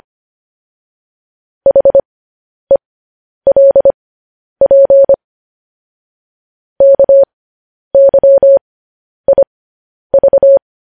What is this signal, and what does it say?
Morse code, heard as a keyed tone, about 25 wpm, spaced as for 10 wpm, 550 Hz. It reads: HELP KYIV
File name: podcast / News headlines-1x-25x10